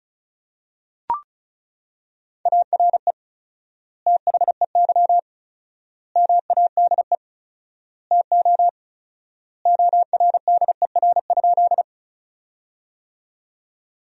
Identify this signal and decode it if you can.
Morse code, heard as a keyed tone, about 36 wpm, 700 Hz. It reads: ARE THEY MADE TO ORDER?
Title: 4x Word-Spacing - Sentences from Top 400 Words - 35wpm